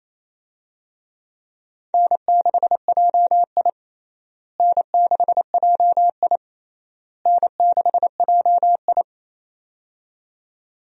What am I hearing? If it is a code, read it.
Morse code, heard as a keyed tone, about 28 wpm, 700 Hz. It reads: N6JI N6JI N6JI
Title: USA Call Signs - 4 Characters - Repeated 3x - 28wpm